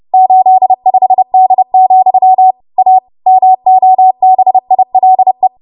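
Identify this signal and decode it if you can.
Morse code, heard as a keyed tone, about 30 wpm, 750 Hz. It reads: 85D, A MOBILE
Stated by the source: Morse practice transmission at 30 WPM.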